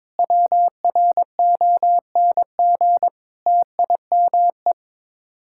Morse code, heard as a keyed tone, about 22 wpm, 700 Hz. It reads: WRONG TIME
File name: practice / 22wpm / Sets of 8 Words - 22wpm